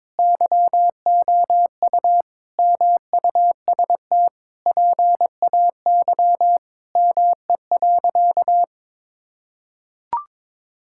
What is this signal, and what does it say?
Morse code, heard as a keyed tone, about 22 wpm, 700 Hz. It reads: YOU MUST PAY ME.
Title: Sentences from Top 1000 Words - Review Part 5 - 22wpm